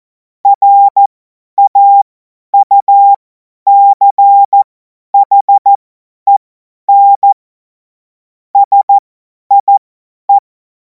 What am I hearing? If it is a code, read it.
Morse code, heard as a keyed tone, about 14 wpm, spaced as for 10 wpm, 800 Hz. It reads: RAUCHEN SIE
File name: Morsekurs / Witze / Witz 18. August 2026 50 F70